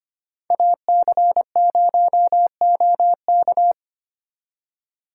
Morse code, heard as a keyed tone, about 32 wpm, 700 Hz. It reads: AC0OK